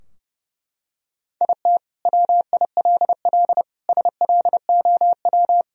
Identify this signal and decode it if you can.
Morse code, heard as a keyed tone, about 32 wpm, 700 Hz. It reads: IT WILL SLOW
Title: Sentences from Top 1000 Words - Review Part 4 - 30wpm